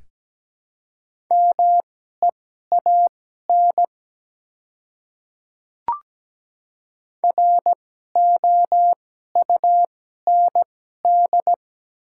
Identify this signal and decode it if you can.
Morse code, heard as a keyed tone, about 17 wpm, spaced as for 12 wpm, 700 Hz. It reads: MEAN ROUND
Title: 2x Character Spacing - Top 200 Words - 17wpm